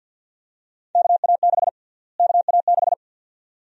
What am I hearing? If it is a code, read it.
Morse code, heard as a keyed tone, about 52 wpm, 700 Hz. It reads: KA6 KA6
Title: USA Call Sign Prefixes - Repeated 2x - 50wpm